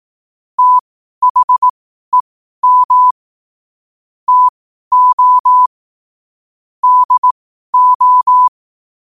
Morse code, heard as a keyed tone, about 18 wpm, spaced as for 12 wpm, 1000 Hz. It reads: THEM TO DO
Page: 12-wpm-morse-code-quotes